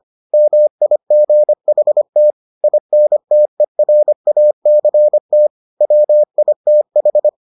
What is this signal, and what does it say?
Morse code, heard as a keyed tone, about 25 wpm, 600 Hz. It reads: MIGHT INTERACT WITH